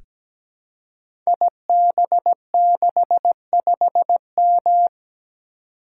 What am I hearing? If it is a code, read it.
Morse code, heard as a keyed tone, about 17 wpm, 700 Hz. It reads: IB65M
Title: Code Groups - 5 Letter:Numbers - 17wpm